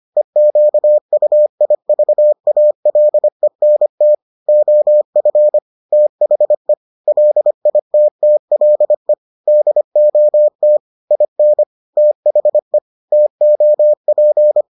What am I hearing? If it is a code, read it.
Morse code, heard as a keyed tone, about 25 wpm, 600 Hz. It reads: EQUIVALENT OF THE LITTLE DOT IN THE TOP